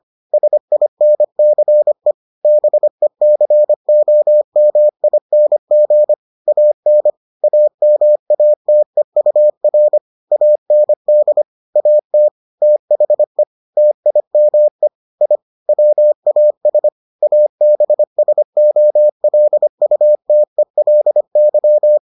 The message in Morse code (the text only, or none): SINCE BECOMING AN AMATEUR AND AT THE TIME I WAS ABSOLUTELY